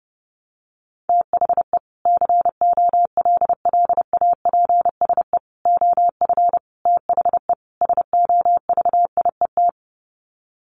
Morse code, heard as a keyed tone, about 30 wpm, 700 Hz. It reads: THE COLLAPSE OF THE SOVIET
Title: Sets of 5 Words Encore 30wpm